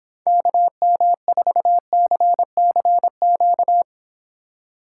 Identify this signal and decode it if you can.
Morse code, heard as a keyed tone, about 26 wpm, 700 Hz. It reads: KM4CCQ